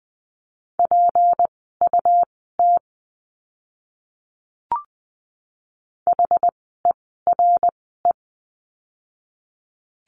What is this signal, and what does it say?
Morse code, heard as a keyed tone, about 20 wpm, spaced as for 14 wpm, 700 Hz. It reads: PUT HERE
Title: 192 Lesson 31f - Letter v - Words 20wpm